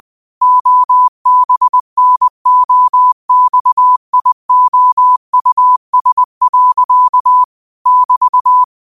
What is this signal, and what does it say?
Morse code, heard as a keyed tone, about 20 wpm, 1000 Hz. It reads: OBNOXIOUS. <BT>
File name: qotdfiles / 20wpm